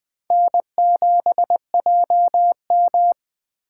Morse code, heard as a keyed tone, about 20 wpm, 700 Hz. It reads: N7JM